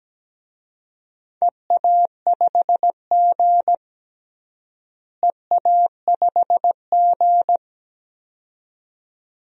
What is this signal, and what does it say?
Morse code, heard as a keyed tone, about 17 wpm, 700 Hz. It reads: EA5G EA5G
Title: DXCC-100 Call Signs - 4 Characters - Repeated 2x - 17wpm